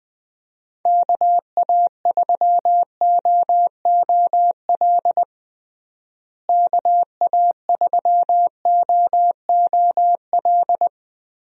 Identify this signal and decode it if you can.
Morse code, heard as a keyed tone, about 20 wpm, 700 Hz. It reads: KA3OOL KA3OOL
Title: USA Call Signs - 6 Characters - Repeated 3x - 20wpm